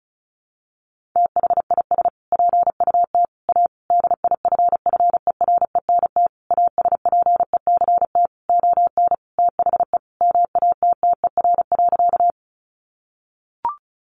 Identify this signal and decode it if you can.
Morse code, heard as a keyed tone, about 34 wpm, 700 Hz. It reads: THIS PUT A DIFFERENT ASPECT ON THE MATTER.
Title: Sentences from Top 1300 Words 35wpm